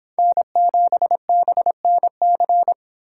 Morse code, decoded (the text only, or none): N7BNC